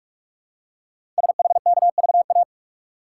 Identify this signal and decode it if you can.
Morse code, heard as a keyed tone, about 44 wpm, 700 Hz. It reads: SHKVA
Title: Code Groups - 5 Letters - 45wpm